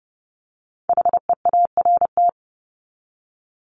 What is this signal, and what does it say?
Morse code, heard as a keyed tone, about 30 wpm, 700 Hz. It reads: HEART
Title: Top 1001-1250 Words 30wpm